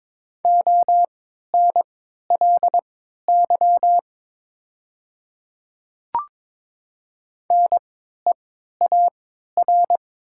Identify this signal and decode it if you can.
Morse code, heard as a keyed tone, about 22 wpm, spaced as for 13 wpm, 700 Hz. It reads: ONLY NEAR
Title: 3x Character Spacing - Top 200 Words - 22wpm